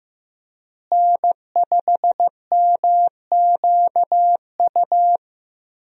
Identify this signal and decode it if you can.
Morse code, heard as a keyed tone, about 15 wpm, 700 Hz. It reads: N5MQU
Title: USA Call Signs - 5 Characters - Repeated 2x - 15wpm